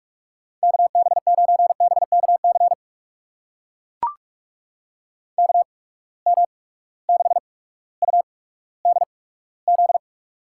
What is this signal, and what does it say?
Morse code, heard as a keyed tone, about 44 wpm, spaced as for 19 wpm, 700 Hz. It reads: KB9BKC KM6UDZ
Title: USA Call Signs - 6 Characters - 8x Character Spacing - 45wpm